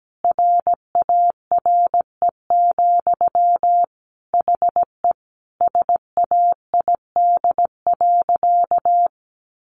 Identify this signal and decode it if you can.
Morse code, heard as a keyed tone, about 17 wpm, 700 Hz. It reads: RARE, HE SAID.
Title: Sentences from Top 1100 Words 17wpm